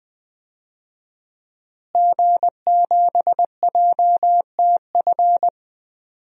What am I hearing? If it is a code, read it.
Morse code, heard as a keyed tone, about 20 wpm, 700 Hz. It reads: G7JTF
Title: DXCC-100 Call Signs - 5 Characters - Repeated 3x - 20wpm